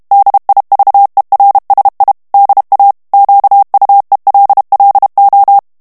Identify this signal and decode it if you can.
Morse code, heard as a keyed tone, about 32 wpm, 800 Hz. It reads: DIVERSI DA QUELLO
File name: ITALIEN texte clair 2 vitesse 30 mots minute